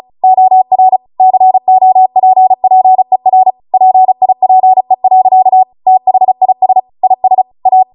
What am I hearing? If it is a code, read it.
Morse code, heard as a keyed tone, about 36 wpm, 750 Hz. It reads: OR COPPER PIPE. THIS IS A